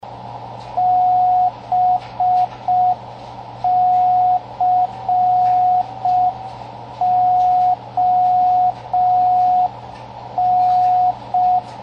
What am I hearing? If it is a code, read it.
Morse code, heard as a keyed tone, about 5 wpm, 730 Hz, with radio noise. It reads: BCON
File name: CWResources / fun stuff / 07 - Slow Beacon